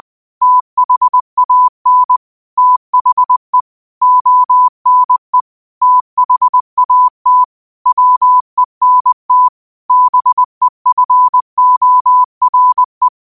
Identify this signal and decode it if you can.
Morse code, heard as a keyed tone, about 20 wpm, 1000 Hz. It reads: THAN THE ONE THAT WENT BEFORE
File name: qotdfiles / 20wpm